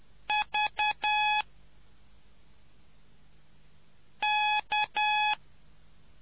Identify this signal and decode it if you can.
Morse code, heard as a keyed tone, about 10 wpm, 840 Hz. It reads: V K